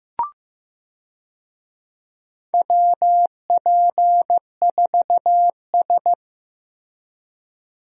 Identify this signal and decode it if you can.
Morse code, heard as a keyed tone, about 15 wpm, 700 Hz. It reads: WP4S
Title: USA Call Signs - 4 Characters - Repeated 3x - 15wpm